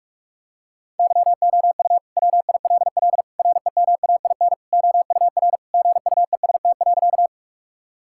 Morse code, heard as a keyed tone, about 46 wpm, 700 Hz. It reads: YOU WILL REMAIN OUR GUEST.
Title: Sentences from Top 1600 Words 45wpm